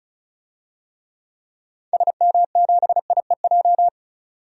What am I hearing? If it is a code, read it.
Morse code, heard as a keyed tone, about 35 wpm, 700 Hz. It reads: SM7IEJ
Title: DXCC-100 Call Signs - 6 Characters - 35wpm